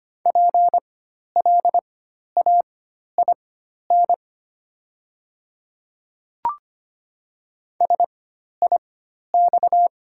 Morse code, heard as a keyed tone, about 25 wpm, spaced as for 12 wpm, 700 Hz. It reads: PLAIN SIX
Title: 4x Character Spacing - Top 500 Words - 25wpm